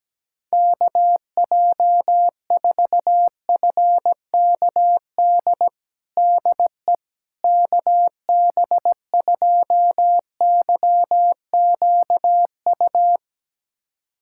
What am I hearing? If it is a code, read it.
Morse code, heard as a keyed tone, about 17 wpm, 700 Hz. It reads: KJ4FKD DE KB2YQU
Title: USA Callsign DE Callsign 17wpm